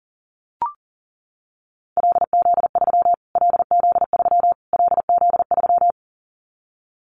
Morse code, heard as a keyed tone, about 40 wpm, 700 Hz. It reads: LZ3 LZ3 LZ3
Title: Callsign Prefixes for Top 25 DXCC - Repeated 3x - 40wpm